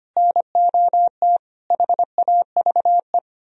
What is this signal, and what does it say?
Morse code, heard as a keyed tone, about 32 wpm, 700 Hz. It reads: NOT HAVE